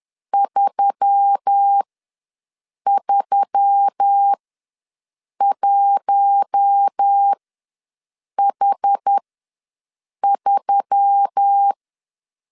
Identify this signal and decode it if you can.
Morse code, heard as a keyed tone, about 10 wpm, spaced as for 6 wpm, 800 Hz. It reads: 331H3
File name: LEZIONI / 5 LBVC / 5 L ws 10wpm 3636